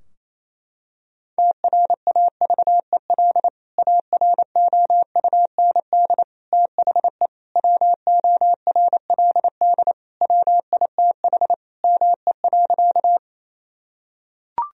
Morse code, heard as a keyed tone, about 28 wpm, 700 Hz. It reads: TRAVEL AROUND THE WORLD WITH ME.
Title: Sentences from Top 1000 Words - Review Part 3 - 28wpm